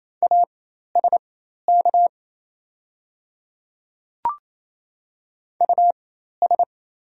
Morse code, heard as a keyed tone, about 28 wpm, spaced as for 13 wpm, 700 Hz. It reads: ASK US
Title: 4x Character Spacing - Top 200 Words - 28wpm